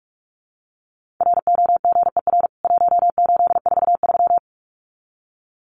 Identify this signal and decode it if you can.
Morse code, heard as a keyed tone, about 44 wpm, 700 Hz. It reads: ROGER 1843
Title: CWT Exchange 45wpm